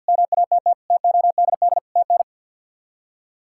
Morse code, heard as a keyed tone, about 49 wpm, 700 Hz. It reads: MATT TODD TN
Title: K1USN SST QSO Practice - 50wpm